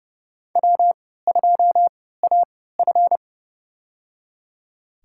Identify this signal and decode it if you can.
Morse code, heard as a keyed tone, about 30 wpm, spaced as for 17 wpm, 700 Hz. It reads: W2AF
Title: USA Call Signs - 3x Character Spacing - 30wpm